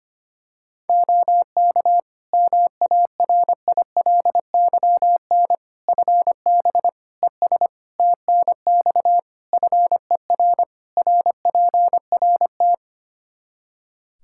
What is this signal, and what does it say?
Morse code, heard as a keyed tone, about 25 wpm, 700 Hz. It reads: OK MARILYN FB ES TNX FER RPRT
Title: Practice QSO - Exchange 2 - Info - 25wpm